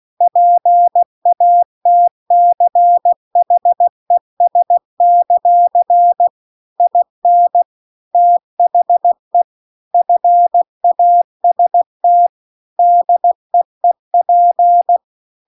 Morse code, heard as a keyed tone, about 16 wpm, 700 Hz. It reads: PATCHES; IN THE FAST DEEP
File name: CW-The-Big-Twoheaded-River-2-16WPM